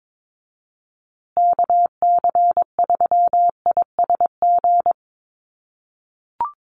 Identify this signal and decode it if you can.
Morse code, heard as a keyed tone, about 22 wpm, 700 Hz. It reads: KC3ISG
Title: USA Call Signs Repeated 3x 22wpm